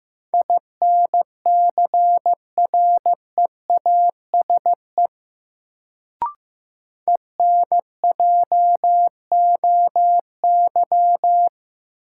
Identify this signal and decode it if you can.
Morse code, heard as a keyed tone, about 15 wpm, 700 Hz. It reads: INCREASE ENJOY
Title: Sentences from Top 1000 Words - Review Part 4 - New Word Drill - 15wpm